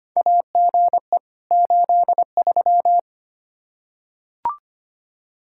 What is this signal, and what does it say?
Morse code, heard as a keyed tone, about 25 wpm, 700 Hz. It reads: AGE 83
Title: Practice QSO - Exchange 3 - More Info - 25wpm